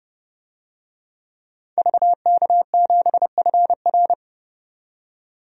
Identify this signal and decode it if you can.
Morse code, heard as a keyed tone, about 30 wpm, 700 Hz. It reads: VK7FR
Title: DXCC-100 Call Signs - 5 Characters - 30wpm